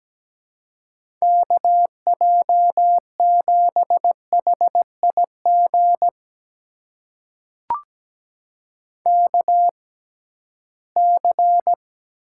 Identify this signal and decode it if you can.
Morse code, heard as a keyed tone, about 17 wpm, spaced as for 13 wpm, 700 Hz. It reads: KJ7HIG KC
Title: USA Call Signs - 6 Characters - 6x Character Spacing - 17wpm